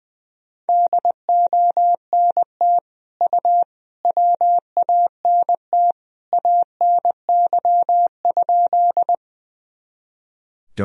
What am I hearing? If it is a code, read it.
Morse code, heard as a keyed tone, about 21 wpm, 700 Hz. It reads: DONT U WANT ANY?
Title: Sentences from Top 300 Words and Abbreviations 20wpm